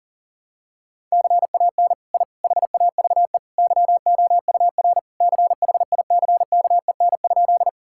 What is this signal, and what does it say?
Morse code, heard as a keyed tone, about 40 wpm, 700 Hz. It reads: CAN I HAVE YOUR CHICKEN?
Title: Sentences from Top 1000 Words - Review Part 5 - 40wpm